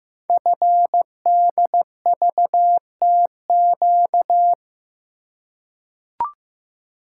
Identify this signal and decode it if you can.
Morse code, heard as a keyed tone, about 15 wpm, 700 Hz. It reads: FDVTQ
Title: Code Groups - 5 Letters - 15wpm